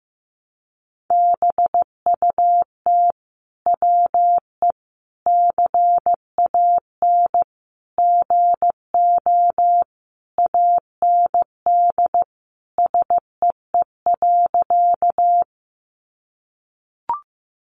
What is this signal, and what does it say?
Morse code, heard as a keyed tone, about 15 wpm, 700 Hz. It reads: BUT WE CAN GO AND SEE.
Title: Sentences from Top 100 Words 15wpm